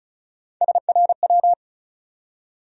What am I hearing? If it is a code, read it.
Morse code, heard as a keyed tone, about 36 wpm, 700 Hz. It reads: SRW
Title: USA Call Sign Postfixes - 3 Letters 35wpm